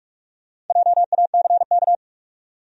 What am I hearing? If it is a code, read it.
Morse code, heard as a keyed tone, about 47 wpm, 700 Hz. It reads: JACK
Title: USA Common Names - Max 5 Letters - 45wpm